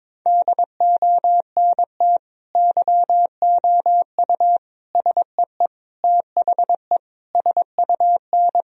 Morse code, heard as a keyed tone, about 21 wpm, 700 Hz. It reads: DONT YOU SEE THE SUN
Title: Sentences from Top 1000 Words - Review Part 2 - 22wpm